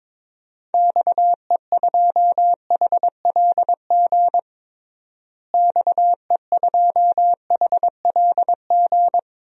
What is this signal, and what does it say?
Morse code, heard as a keyed tone, about 21 wpm, 700 Hz. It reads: XE2HLG XE2HLG
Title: DXCC-100 Call Signs - 6 Characters - Repeated 2x - 22wpm